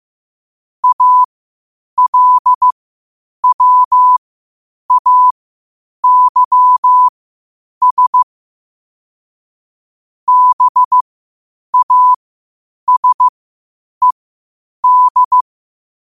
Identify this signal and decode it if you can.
Morse code, heard as a keyed tone, about 15 wpm, spaced as for 8 wpm, 1000 Hz. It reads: ALWAYS BASED
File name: qotdfiles / 8wpm